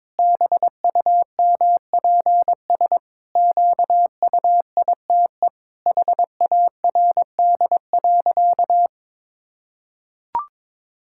Morse code, heard as a keyed tone, about 21 wpm, 700 Hz. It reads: BUMPS QUITE HARD.
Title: Pangram Challenge v2 - 22wpm